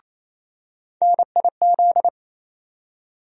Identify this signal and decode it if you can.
Morse code, heard as a keyed tone, about 28 wpm, 700 Hz. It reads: NIZ
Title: USA Call Sign Postfixes - 3 Letters 28wpm